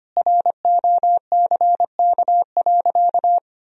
Morse code, heard as a keyed tone, about 25 wpm, 700 Hz. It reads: ROCK.
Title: Sentences from Top 1000 Words - Review Part 2 - 25wpm